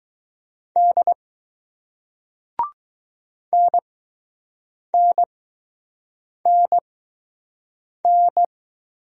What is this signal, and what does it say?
Morse code, heard as a keyed tone, about 20 wpm, spaced as for 6 wpm, 700 Hz. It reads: D NNNN